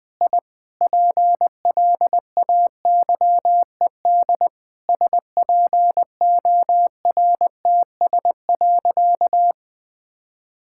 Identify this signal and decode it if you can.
Morse code, heard as a keyed tone, about 20 wpm, 700 Hz. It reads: I PLAYED SPORTS.
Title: Sentences with 6-Word Phrases - 20wpm